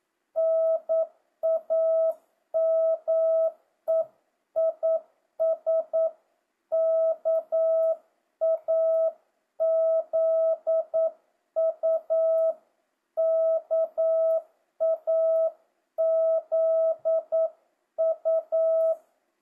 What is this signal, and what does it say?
Morse code, heard as a keyed tone, about 9 wpm, 650 Hz. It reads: NAME IS KAZU KAZU
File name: jh8cbh / 2025cw